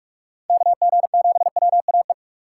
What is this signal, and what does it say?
Morse code, heard as a keyed tone, about 40 wpm, 700 Hz. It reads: KG7WAE